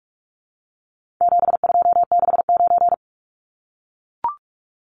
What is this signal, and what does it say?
Morse code, heard as a keyed tone, about 44 wpm, 700 Hz. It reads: 7269
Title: Numbers - 4-Digit 45wpm